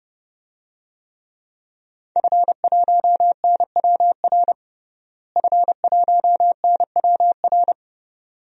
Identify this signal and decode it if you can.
Morse code, heard as a keyed tone, about 30 wpm, 700 Hz. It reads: F1NWR F1NWR
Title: DXCC-100 Call Signs - 5 Characters - Repeated 2x - 30wpm